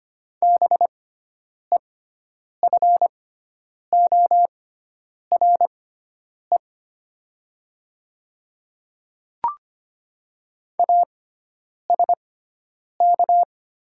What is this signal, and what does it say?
Morse code, heard as a keyed tone, about 25 wpm, spaced as for 9 wpm, 700 Hz. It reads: BEFORE ASK
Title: 6x Character Spacing - Top 300 Words - 25wpm